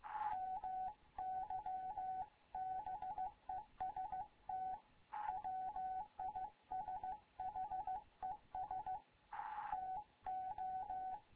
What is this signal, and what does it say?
Morse code, heard as a keyed tone, about 15 wpm, 750 Hz. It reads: MY BEST WISHES TO
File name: Morze-HA2MN / hangp / 2 11 GYAKORLAT